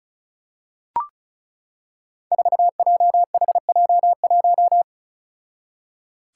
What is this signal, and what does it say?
Morse code, heard as a keyed tone, about 36 wpm, 700 Hz. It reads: 4JHJ1
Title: Five Challenging Letter-Number Combinations 35wpm